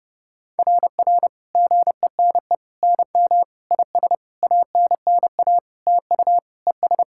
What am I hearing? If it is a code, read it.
Morse code, heard as a keyed tone, about 30 wpm, 690 Hz. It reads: RR GENE NM IS ANNA TU ES
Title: SOTA QSO Practice 30wpm